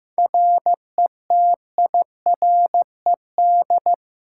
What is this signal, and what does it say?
Morse code, heard as a keyed tone, about 15 wpm, 700 Hz. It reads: RETIRED
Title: Practice QSO - Exchange 3 - More Info - 15wpm